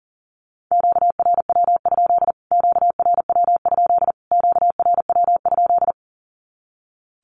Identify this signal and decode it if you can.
Morse code, heard as a keyed tone, about 40 wpm, 700 Hz. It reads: QRW? QRW? QRW?
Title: Q-Codes Repeated 3x 40wpm